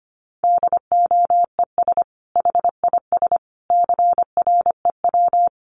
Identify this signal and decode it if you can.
Morse code, heard as a keyed tone, about 25 wpm, 700 Hz. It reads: DOES HIS CREW
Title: Sentences from Top 1500 Words 25wpm